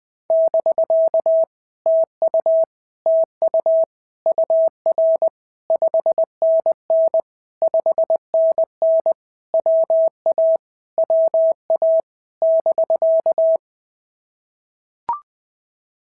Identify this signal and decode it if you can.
Morse code, heard as a keyed tone, about 20 wpm, 650 Hz. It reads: <BK> TU TU UR 5NN 5NN WA WA <BK>
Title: POTA QSO Practice - 20wpm